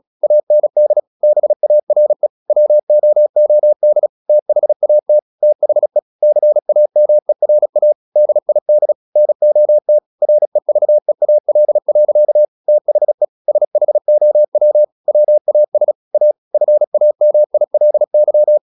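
Morse code, heard as a keyed tone, about 36 wpm, 600 Hz. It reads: AND BARE WOOD THAT THE CAMERA DID NOT REVEAL. THE SHOW WAS A FAMILY
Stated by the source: Morse practice at 36 wpm, text of Ken Follett's novel Never, from CW med Gnister.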